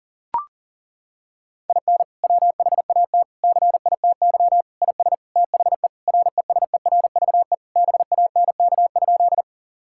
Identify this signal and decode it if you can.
Morse code, heard as a keyed tone, about 40 wpm, 700 Hz. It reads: IN WHAT CITY IS THE RESERVE BANK?
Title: Sentences from Top 1000 Words - Review Part 6 - 40wpm